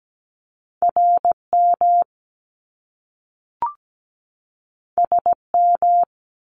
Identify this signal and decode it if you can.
Morse code, heard as a keyed tone, about 17 wpm, 700 Hz. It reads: RM SM
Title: Top Two Letter Combinations in Words 17wpm